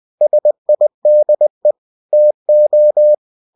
Morse code, heard as a keyed tone, about 20 wpm, 600 Hz. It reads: SIDE TO
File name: cw-books / winnie-the-pooh / winnie-20wpm-page-0047